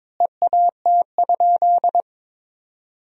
Morse code, heard as a keyed tone, about 22 wpm, 700 Hz. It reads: EAT?
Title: 2x Word-Spacing - Sentences from Top 500 Words - 22wpm